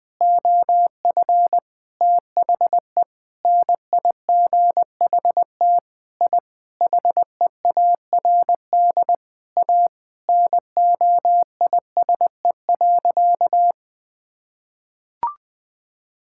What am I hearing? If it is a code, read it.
Morse code, heard as a keyed tone, about 20 wpm, 700 Hz. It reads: OF THE NIGHT I HEARD A NOISE.
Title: Sentences with 5-Word Phrases - 20wpm